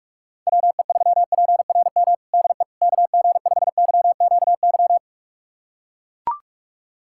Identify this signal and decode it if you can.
Morse code, heard as a keyed tone, about 44 wpm, 700 Hz. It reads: WE3PRM DE KG5YQY
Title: USA Callsign DE Callsign - 6 Characters - 45wpm